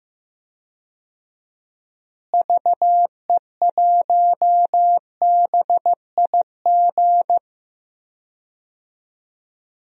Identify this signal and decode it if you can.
Morse code, heard as a keyed tone, about 15 wpm, 700 Hz. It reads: VE1BIG
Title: DXCC-100 Call Signs - 6 Characters - Repeated 3x - 15wpm